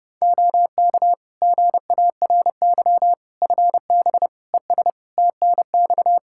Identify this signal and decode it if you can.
Morse code, heard as a keyed tone, about 30 wpm, 700 Hz. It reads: OK GARY FB ES TNX
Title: Practice QSO - Exchange 2 - Info - 30wpm